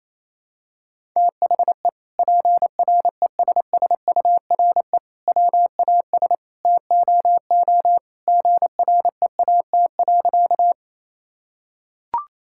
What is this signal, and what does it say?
Morse code, heard as a keyed tone, about 28 wpm, 700 Hz. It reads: THE PRESSURE WAS TOO GREAT.
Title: Sentences from Top 1000 Words - Review Part 6 - 28wpm